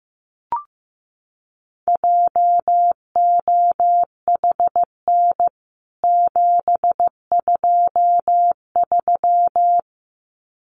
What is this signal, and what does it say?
Morse code, heard as a keyed tone, about 15 wpm, 700 Hz. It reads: JOHN 723
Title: CWT Exchange 15wpm